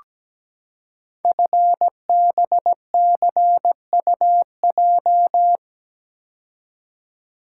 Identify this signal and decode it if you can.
Morse code, heard as a keyed tone, about 17 wpm, 700 Hz. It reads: FBCUJ
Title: Code Groups - 5 Letters - 17wpm